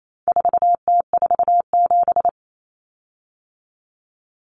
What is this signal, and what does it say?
Morse code, heard as a keyed tone, about 28 wpm, 700 Hz. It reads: 4T47
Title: Common Cut Numbers - 4-Digit 28wpm